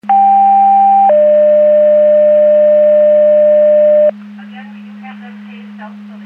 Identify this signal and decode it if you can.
Two-tone page, heard as a one-second tone, then a three-second tone, about 790 then 600 Hz, with speech. It is heard from 0.1 to 4.1 s.